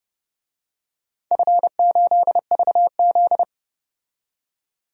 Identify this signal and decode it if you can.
Morse code, heard as a keyed tone, about 32 wpm, 700 Hz. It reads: F8VZ
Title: DXCC-100 Call Signs - 4 Characters - 30wpm